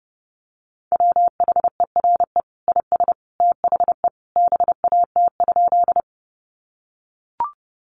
Morse code, heard as a keyed tone, about 28 wpm, 700 Hz. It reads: WHERE IS THE BAT?
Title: Sentences from Top 1500 Words 30wpm